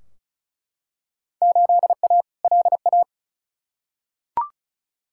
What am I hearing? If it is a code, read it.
Morse code, heard as a keyed tone, about 34 wpm, 700 Hz. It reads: 8A LA
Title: Field Day Exchanges Encore - 35wpm